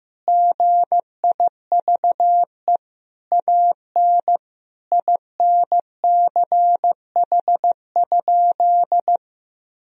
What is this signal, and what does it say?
Morse code, heard as a keyed tone, about 16 wpm, 700 Hz. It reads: GIVE AN INCH?
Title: Sentences from Top 1000 Words - Review Part 3 - 15wpm